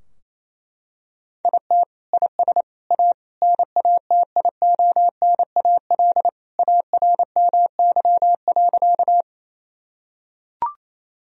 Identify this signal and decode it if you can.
Morse code, heard as a keyed tone, about 28 wpm, 700 Hz. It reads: IT IS A NATIONAL ARMY.
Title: Sentences from Top 1000 Words - Review Part 4 - 28wpm